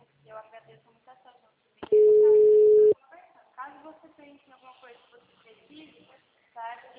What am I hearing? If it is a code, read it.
Ringback tone (Europe, 425 Hz), with speech.